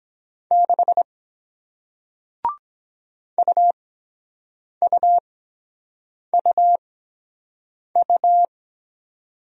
Morse code, heard as a keyed tone, about 23 wpm, spaced as for 7 wpm, 700 Hz. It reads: 6 UUUU